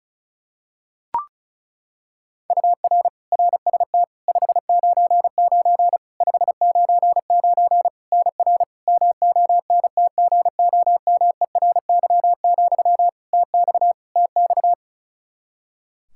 Morse code, heard as a keyed tone, about 34 wpm, 700 Hz. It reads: UR RST 599 599 NR MONTGOMERY, TX TX
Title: Practice QSO - Exchange 1 - Beginning - 35wpm